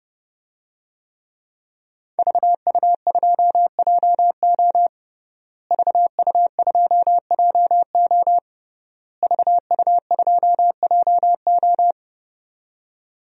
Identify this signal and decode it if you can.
Morse code, heard as a keyed tone, about 30 wpm, 700 Hz. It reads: VU2JO VU2JO VU2JO
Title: DXCC-100 Call Signs - 5 Characters - Repeated 3x - 30wpm